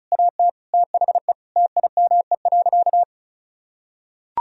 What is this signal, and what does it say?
Morse code, heard as a keyed tone, about 36 wpm, 700 Hz. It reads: AT THE TIME.
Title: Sentences from Top 1000 Words - Review Part 2 - 35wpm